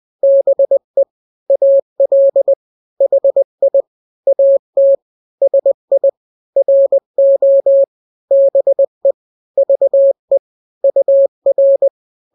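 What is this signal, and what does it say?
Morse code, heard as a keyed tone, about 20 wpm, 550 Hz. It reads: BE AL HI AT SI RO BE VE UR
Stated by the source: Morse drill of two-letter n-grams at 20/18 WPM (Farnsworth), from KK7RR.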